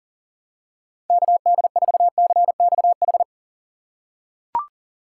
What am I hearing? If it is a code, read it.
Morse code, heard as a keyed tone, about 40 wpm, 700 Hz. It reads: KD4CXH
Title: USA Call Signs - 6 Characters - 3x Character Spacing - 40wpm